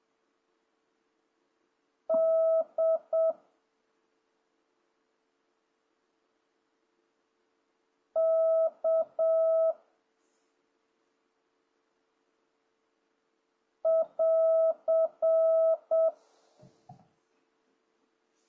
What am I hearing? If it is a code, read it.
Morse code, heard as a keyed tone, about 7 wpm, 650 Hz. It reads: D K <AR>
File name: jh8cbh / 2026cw